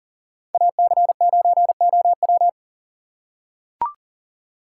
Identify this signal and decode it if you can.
Morse code, heard as a keyed tone, about 40 wpm, 700 Hz. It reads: AC9OW
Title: USA Call Signs - 5 Characters - 2x Character Spacing - 40wpm